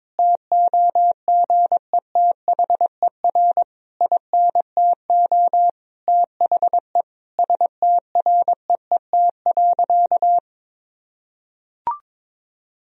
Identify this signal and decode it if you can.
Morse code, heard as a keyed tone, about 22 wpm, 700 Hz. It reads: TOGETHER INTO THE STREET.
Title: Sentences from Top 1000 Words - Review Part 3 - 22wpm